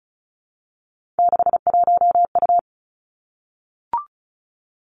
Morse code, heard as a keyed tone, about 36 wpm, 700 Hz. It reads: 61U
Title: Three Challenging Letter-Number Combinations 35wpm